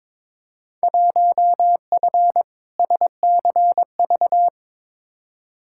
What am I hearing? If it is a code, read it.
Morse code, heard as a keyed tone, about 21 wpm, 700 Hz. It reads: 1F SCV
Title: Field Day Exchanges Encore - 22wpm